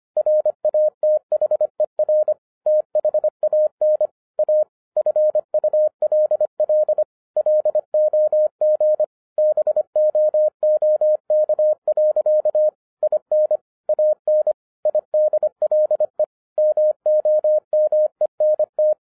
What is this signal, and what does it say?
Morse code, heard as a keyed tone, about 25 wpm, 610 Hz. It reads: RATHER THAN A FULL LOG BOOK. IN AN IDLE MOMENT
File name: 2012-01-mp3 / n14-FUN WITH NUMBERS-25wpm